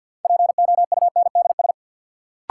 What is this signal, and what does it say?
Morse code, heard as a keyed tone, about 49 wpm, 700 Hz. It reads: POUNDS
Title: Sentences from Top 1000 Words - Review Part 2 - New Word Drill - 50wpm